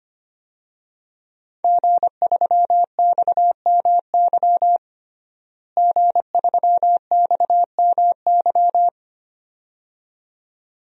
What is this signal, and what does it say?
Morse code, heard as a keyed tone, about 25 wpm, 700 Hz. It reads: G3XMY G3XMY
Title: DXCC-100 Call Signs - 5 Characters - Repeated 2x - 25wpm